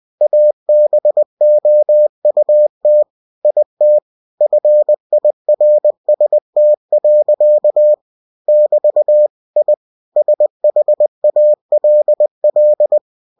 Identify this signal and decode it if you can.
Morse code, heard as a keyed tone, about 20 wpm, 600 Hz. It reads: ABOUT IT FIRST. <BT> I SHALL
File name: cw-books / winnie-the-pooh / winnie-20wpm-page-0034